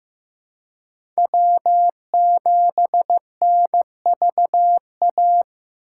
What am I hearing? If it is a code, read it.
Morse code, heard as a keyed tone, about 15 wpm, 700 Hz. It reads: W7NVA
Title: USA Call Signs - 5 Characters - 8x Character Spacing - 15wpm